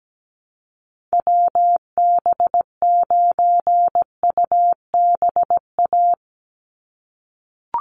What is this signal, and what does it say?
Morse code, heard as a keyed tone, about 17 wpm, 700 Hz. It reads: WB9UBA
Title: USA Call Signs 17wpm